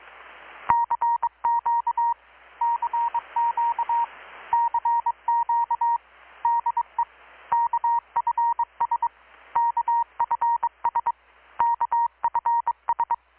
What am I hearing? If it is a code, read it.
Morse code, heard as a keyed tone, about 22 wpm, 940 Hz. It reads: CQ CQ CQ DE KFS KFS KFS
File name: blog / swl / 6-abril-2019-12695-USB-KPS